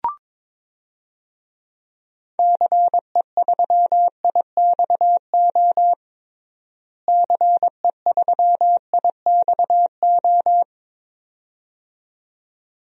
Morse code, heard as a keyed tone, about 22 wpm, 700 Hz. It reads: CE3IXO CE3IXO
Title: DXCC-100 Call Signs - 6 Characters - Repeated 2x - 22wpm